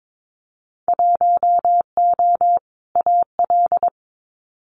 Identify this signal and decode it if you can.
Morse code, heard as a keyed tone, about 22 wpm, 700 Hz. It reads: 1O AL
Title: Winter Field Day Exchanges 22wpm